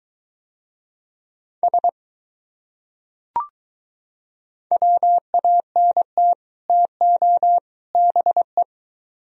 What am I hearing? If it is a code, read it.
Morse code, heard as a keyed tone, about 23 wpm, 700 Hz. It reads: S WANT TO BE